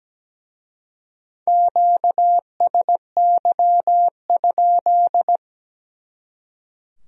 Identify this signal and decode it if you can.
Morse code, heard as a keyed tone, about 20 wpm, 700 Hz. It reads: QSY?
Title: Speed-Racing Q-Codes - 17wpm